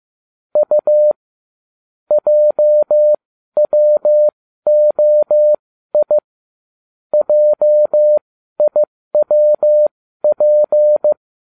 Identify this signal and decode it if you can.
Morse code, heard as a keyed tone, about 15 wpm, spaced as for 12 wpm, 600 Hz. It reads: U JWOI JIWP